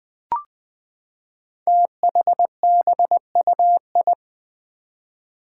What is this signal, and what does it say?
Morse code, heard as a keyed tone, about 20 wpm, 700 Hz. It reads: THBUI
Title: Five Challenging Letter-Number Combinations 20wpm